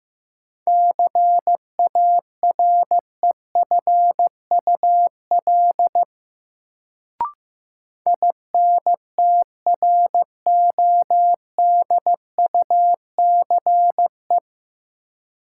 Morse code, heard as a keyed tone, about 15 wpm, 700 Hz. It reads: CAREFUL INTRODUCE
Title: Sentences from Top 1600 Words - New Word Drill Alt - 15wpm